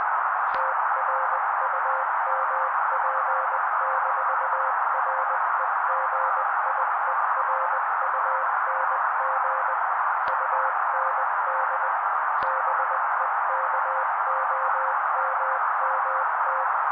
Morse code, heard as a keyed tone, about 20 wpm, 530 Hz. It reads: TRUMP-REGIERUNG UND BEKOMMT
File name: daily / 18wpm noise